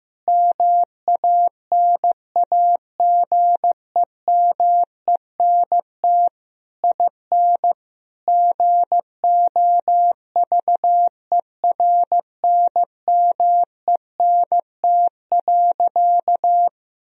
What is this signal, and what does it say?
Morse code, heard as a keyed tone, about 15 wpm, 700 Hz. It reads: MANAGEMENT IN GOVERNMENT.
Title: Sentences from Top 1000 Words - Review Part 4 - 15wpm